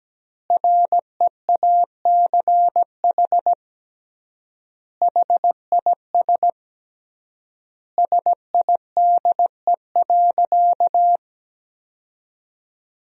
Morse code, heard as a keyed tone, about 17 wpm, 700 Hz. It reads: REACH HIS SIDE.
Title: 3x Word-Spacing - Sentences from Top 400 Words - 17wpm